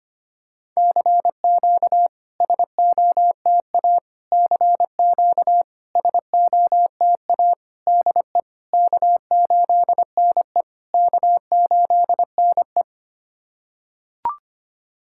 Morse code, heard as a keyed tone, about 25 wpm, 700 Hz. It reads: CQ SOTA CQ SOTA DE K8NE K8NE
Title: SOTA QSO Practice 25wpm